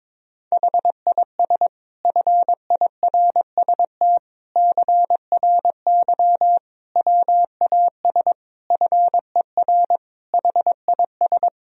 Morse code, heard as a keyed tone, about 22 wpm, 700 Hz. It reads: HIS FIRST CRY WAS FER HIS
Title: Sentences from Top 500 Words and Abbreviations 22wpm